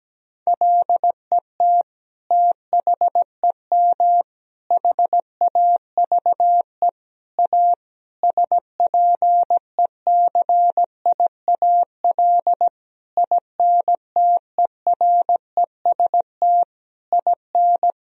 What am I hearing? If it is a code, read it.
Morse code, heard as a keyed tone, about 17 wpm, 700 Hz. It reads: LET THEM HAVE A SPECIAL INTEREST IN
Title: Sentences from Top 1000 Words - Review Part 3 - 17wpm